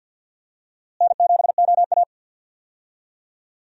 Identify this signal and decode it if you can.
Morse code, heard as a keyed tone, about 53 wpm, 700 Hz. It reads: N7OA
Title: USA Call Signs - 4 Characters - Repeated 3x - 50wpm